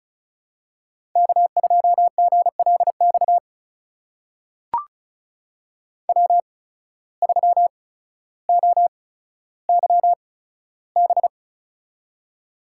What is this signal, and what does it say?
Morse code, heard as a keyed tone, about 36 wpm, spaced as for 15 wpm, 700 Hz. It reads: K2GLX W3OYB
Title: USA Call Signs - 5 Characters - 8x Character Spacing - 35wpm